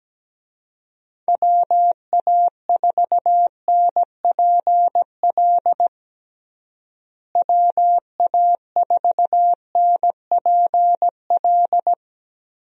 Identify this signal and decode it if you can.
Morse code, heard as a keyed tone, about 17 wpm, 700 Hz. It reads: WA4NPL WA4NPL
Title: USA Call Signs - 6 Characters - Repeated 3x - 17wpm